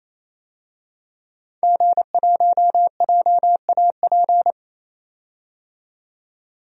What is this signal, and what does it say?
Morse code, heard as a keyed tone, about 28 wpm, 700 Hz. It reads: G1JAP
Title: DXCC-100 Call Signs - 5 Characters - 28wpm